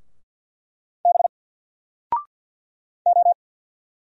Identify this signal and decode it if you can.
Morse code, heard as a keyed tone, about 50 wpm, 700 Hz. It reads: B O
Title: Single Letter-Number 50wpm